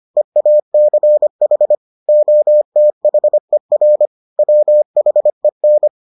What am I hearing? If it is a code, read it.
Morse code, heard as a keyed tone, about 25 wpm, 600 Hz. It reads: EACH OTHER WHEN